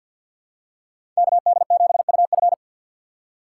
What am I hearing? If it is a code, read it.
Morse code, heard as a keyed tone, about 47 wpm, 700 Hz. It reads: KD7UF
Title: USA Call Signs - 4x Character Spacing - 50wpm